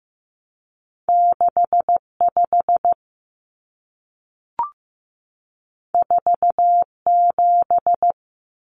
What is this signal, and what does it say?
Morse code, heard as a keyed tone, about 15 wpm, 700 Hz. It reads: 65 47
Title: Numbers - 2-Digit 15wpm